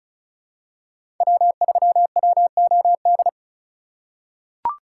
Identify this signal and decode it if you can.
Morse code, heard as a keyed tone, about 36 wpm, 700 Hz. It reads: W3WOD
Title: USA Call Signs - 5 Characters - 2x Character Spacing - 35wpm